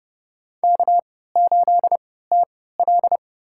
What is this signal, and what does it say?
Morse code, heard as a keyed tone, about 30 wpm, spaced as for 17 wpm, 700 Hz. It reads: K8TL
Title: USA Call Signs - 3x Character Spacing - 30wpm